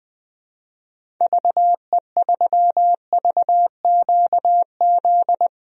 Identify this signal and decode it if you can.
Morse code, heard as a keyed tone, about 21 wpm, 700 Hz. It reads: VE3VQZ
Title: DXCC-100 Call Signs - 6 Characters - Repeated 2x - 20wpm